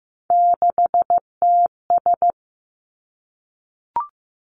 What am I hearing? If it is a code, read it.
Morse code, heard as a keyed tone, about 15 wpm, 700 Hz. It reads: 6TS
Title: Three Challenging Letter-Number Combinations 15wpm